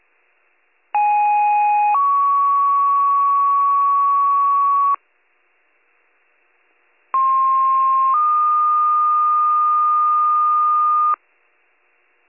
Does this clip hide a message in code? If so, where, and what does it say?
Yes: two-tone page from 0.9 to 11.1 s, heard as a one-second tone, then a three-second tone, about 830 then 1120 Hz, about 1060 then 1220 Hz.